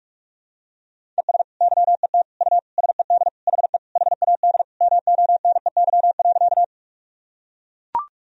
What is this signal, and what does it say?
Morse code, heard as a keyed tone, about 41 wpm, 700 Hz. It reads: ES YET U SED HE HAD MONEY.
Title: Sentences from Top 500 Words and Abbreviations 45wpm